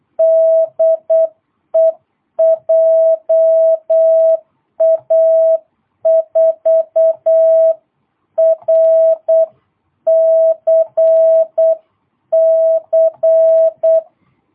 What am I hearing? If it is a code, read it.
Morse code, heard as a keyed tone, about 8 wpm, 660 Hz. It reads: DE JA4RCC
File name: jh8cbh / 2022cw